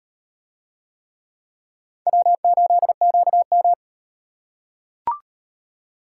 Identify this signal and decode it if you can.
Morse code, heard as a keyed tone, about 38 wpm, 700 Hz. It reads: W8QM